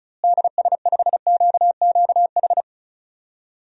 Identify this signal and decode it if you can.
Morse code, heard as a keyed tone, about 36 wpm, 700 Hz. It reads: DS5QQH
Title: DXCC-100 Call Signs - 6 Characters - Repeated 3x - 35wpm